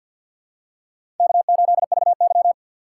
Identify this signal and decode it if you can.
Morse code, heard as a keyed tone, about 46 wpm, 700 Hz. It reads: K8VY
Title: USA Call Signs - 4 Characters - 3x Character Spacing - 50wpm